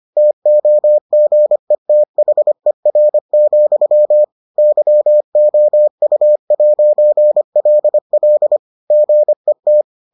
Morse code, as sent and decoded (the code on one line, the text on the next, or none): - --- --. - .... .-. --..-- / -.-- --- ..- .----. .-.. .-.. / --. -
TOGETHER, YOU'LL GET